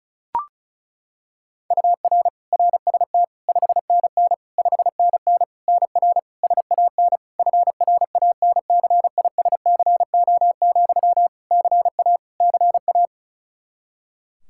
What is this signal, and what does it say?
Morse code, heard as a keyed tone, about 34 wpm, 700 Hz. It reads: UR RST 5NN 5NN NR SAN FRANCISCO, CA CA
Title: Practice QSO - Exchange 1 - Beginning - 35wpm